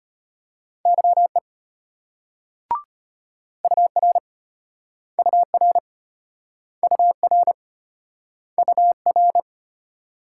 Morse code, heard as a keyed tone, about 34 wpm, 700 Hz. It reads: YE UR UR UR UR